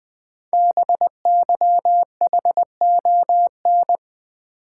Morse code, heard as a keyed tone, about 20 wpm, 700 Hz. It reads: BYHON
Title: Code Groups - 5 Letters - 20wpm